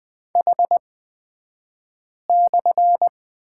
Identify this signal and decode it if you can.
Morse code, heard as a keyed tone, about 20 wpm, spaced as for 5 wpm, 700 Hz. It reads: H/
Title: 223 Lesson 36d - BT - Two Characters 20wpm